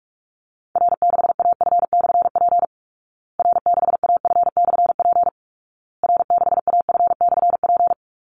Morse code, heard as a keyed tone, about 44 wpm, 700 Hz. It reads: R6AF/P R6AF/P R6AF/P
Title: Call Signs Repeated 2x - 45wpm